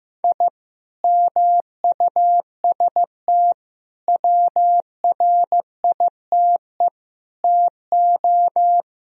Morse code, heard as a keyed tone, about 15 wpm, 700 Hz. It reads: I MUST WRITE TO
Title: Sentences from Top 1000 Words - Review Part 1 - 15wpm